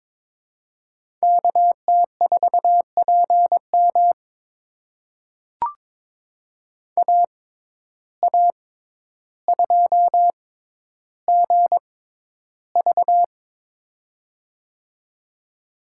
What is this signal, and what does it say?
Morse code, heard as a keyed tone, about 22 wpm, spaced as for 11 wpm, 700 Hz. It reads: KT4PM AA2GV
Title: USA Call Signs - 5 Characters - 6x Character Spacing - 22wpm